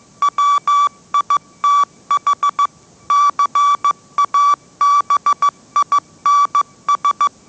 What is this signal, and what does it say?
Morse code, heard as a keyed tone, about 15 wpm, 1150 Hz. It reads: WITH CABINS
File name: mrx / mrx065 16wpm